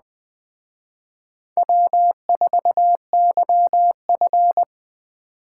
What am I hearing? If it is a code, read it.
Morse code, heard as a keyed tone, about 20 wpm, 700 Hz. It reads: W4YF